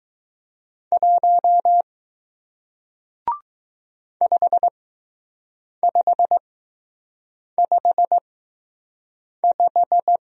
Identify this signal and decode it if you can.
Morse code, heard as a keyed tone, about 20 wpm, spaced as for 6 wpm, 700 Hz. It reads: 1 5555